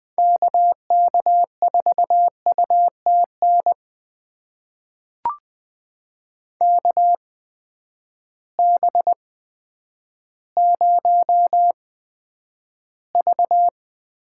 Morse code, heard as a keyed tone, about 20 wpm, spaced as for 10 wpm, 700 Hz. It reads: KK4UTN KB0V
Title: USA Call Signs - 6 Characters - 8x Character Spacing - 20wpm